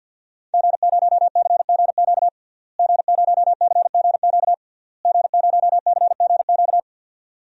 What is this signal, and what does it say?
Morse code, heard as a keyed tone, about 51 wpm, 700 Hz. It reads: G0CGQ G0CGQ G0CGQ
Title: DXCC-100 Call Signs - 5 Characters - Repeated 3x - 50wpm